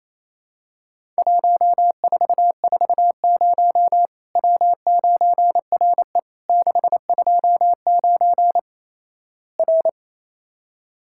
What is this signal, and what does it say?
Morse code, heard as a keyed tone, about 28 wpm, 700 Hz. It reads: 1440 W9RE 629 R
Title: Worked All Europe DX Contest - QTC Exchange - 28wpm